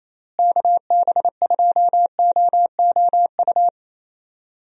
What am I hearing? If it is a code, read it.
Morse code, heard as a keyed tone, about 28 wpm, 700 Hz. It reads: KB2OOU
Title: USA Call Signs - 6 Characters - Repeated 2x - 28wpm